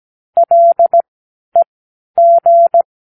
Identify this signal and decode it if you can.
Morse code, heard as a keyed tone, about 17 wpm, spaced as for 10 wpm, 680 Hz. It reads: LEG
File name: k7qo / Practicewords